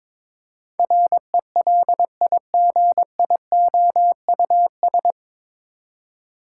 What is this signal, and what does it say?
Morse code, heard as a keyed tone, about 22 wpm, 700 Hz. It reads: RELIGIOUS
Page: Morse Code Ninja practice 22wpm